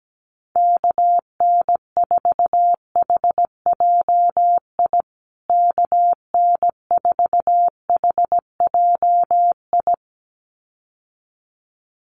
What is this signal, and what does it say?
Morse code, heard as a keyed tone, about 17 wpm, 700 Hz. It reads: KN4HJI KN4HJI
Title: USA Call Signs Repeated 2x 17wpm